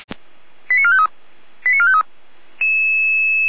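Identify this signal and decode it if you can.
Five-tone selective call (ZVEI), heard as quick runs of five tones, tones of about 70 ms. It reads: 89453 89453
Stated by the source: ZVEI five-tone selective call received with an SDR on a fire brigade channel, directory labelled 455.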